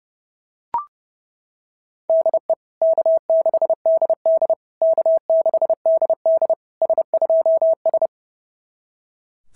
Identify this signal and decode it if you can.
Morse code, heard as a keyed tone, about 30 wpm, 650 Hz. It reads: DE K6DD K6DD S2S
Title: SOTA QSO Practice 30wpm